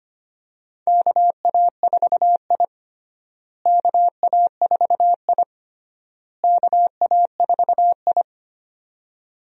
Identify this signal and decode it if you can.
Morse code, heard as a keyed tone, about 25 wpm, 700 Hz. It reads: KA4I KA4I KA4I
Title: USA Call Signs - 4 Characters - Repeated 3x - 25wpm